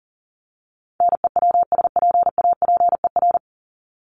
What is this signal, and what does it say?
Morse code, heard as a keyed tone, about 40 wpm, 700 Hz. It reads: NEWSPAPER
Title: Top 751-1000 Words 40wpm